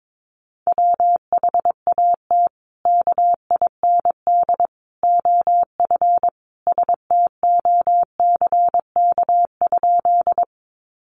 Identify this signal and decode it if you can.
Morse code, heard as a keyed tone, about 22 wpm, 700 Hz. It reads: WHAT KIND OF STOCK?
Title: Sentences from Top 800 Words 22wpm